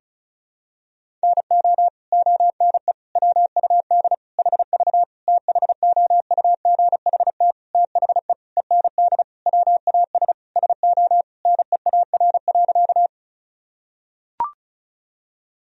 Morse code, heard as a keyed tone, about 36 wpm, 700 Hz. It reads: NO ONE WUD HV THOUGHT THE END WAS SO NEAR.
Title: Sentences from Top 300 Words and Abbreviations 35wpm